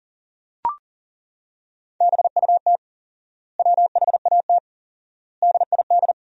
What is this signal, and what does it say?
Morse code, heard as a keyed tone, about 40 wpm, 700 Hz. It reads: BUT WHAT DID
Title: 4x Word-Spacing - Sentences from Top 400 Words - 40wpm